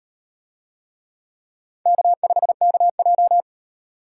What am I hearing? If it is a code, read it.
Morse code, heard as a keyed tone, about 35 wpm, 700 Hz. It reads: K5KJ